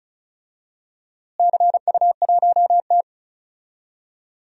Morse code, heard as a keyed tone, about 36 wpm, 700 Hz. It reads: CU1T
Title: DXCC-100 Call Signs - 4 Characters - Repeated 3x - 35wpm